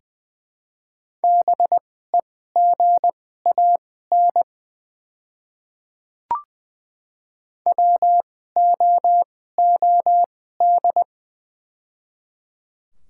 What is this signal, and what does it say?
Morse code, heard as a keyed tone, about 20 wpm, spaced as for 15 wpm, 700 Hz. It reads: BEGAN WOOD
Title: 2x Character Spacing - Top 300 Words - 20wpm